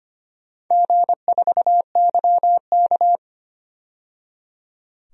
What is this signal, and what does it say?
Morse code, heard as a keyed tone, about 25 wpm, 700 Hz. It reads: G4YK
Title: DXCC-100 Call Signs - 4 Characters - 25wpm